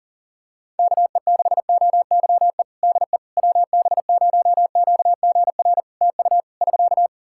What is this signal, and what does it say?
Morse code, heard as a keyed tone, about 40 wpm, 700 Hz. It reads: KE6OYE DE WB0QGR TU <SK>
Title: Practice QSO - Exchange 4 - Ending - 40wpm